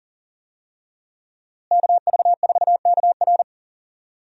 Morse code, heard as a keyed tone, about 40 wpm, 700 Hz. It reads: KV4KR
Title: USA Call Signs - 5 Characters - Repeated 3x - 40wpm